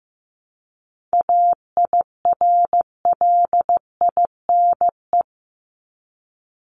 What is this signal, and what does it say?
Morse code, heard as a keyed tone, about 15 wpm, 700 Hz. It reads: AIRLINE
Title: Top 1501-1750 Words 15wpm